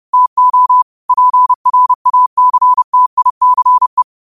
Morse code, heard as a keyed tone, about 30 wpm, 1000 Hz. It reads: TO PRACTICE
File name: qotdfiles / 30wpm